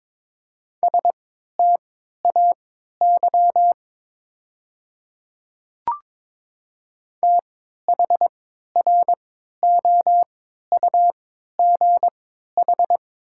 Morse code, heard as a keyed tone, about 23 wpm, spaced as for 13 wpm, 700 Hz. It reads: STAY THROUGH
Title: 3x Character Spacing - Top 500 Words - 22wpm